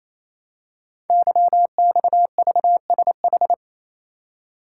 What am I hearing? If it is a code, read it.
Morse code, heard as a keyed tone, about 27 wpm, 700 Hz. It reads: YXVSH
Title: Code Groups - 5 Letter:Numbers - 28wpm